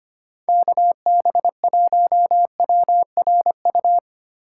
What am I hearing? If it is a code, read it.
Morse code, heard as a keyed tone, about 25 wpm, 700 Hz. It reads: KB1WRU
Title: USA Call Signs - 6 Characters - 3x Character Spacing - 25wpm